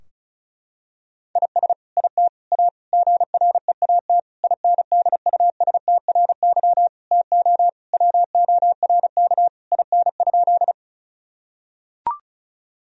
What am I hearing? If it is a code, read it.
Morse code, heard as a keyed tone, about 34 wpm, 700 Hz. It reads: IS IT A GREAT INDUSTRY TO WORK IN?
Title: Sentences from Top 1000 Words - Review Part 4 - 35wpm